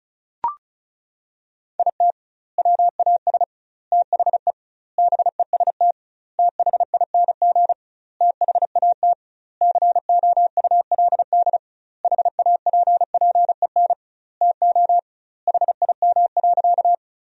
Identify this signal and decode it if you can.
Morse code, heard as a keyed tone, about 36 wpm, 700 Hz. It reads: IT WAS THE BEST THING THAT COULD HAPPEN TO HIM.
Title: 2x Word-Spacing - Sentences from Top 400 Words - 35wpm